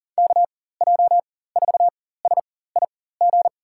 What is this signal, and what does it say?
Morse code, heard as a keyed tone, about 40 wpm, spaced as for 19 wpm, 700 Hz. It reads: KJ4SIG
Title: USA Call Signs - 6 Characters - 4x Character Spacing - 40wpm